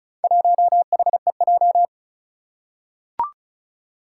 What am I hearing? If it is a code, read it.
Morse code, heard as a keyed tone, about 36 wpm, 700 Hz. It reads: 1HEJ
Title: Four Challenging Letter-Number Combinations 35wpm